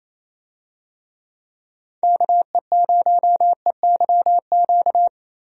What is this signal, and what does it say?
Morse code, heard as a keyed tone, about 27 wpm, 700 Hz. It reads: KE0EYQ
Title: USA Call Signs - 6 Characters - Repeated 2x - 28wpm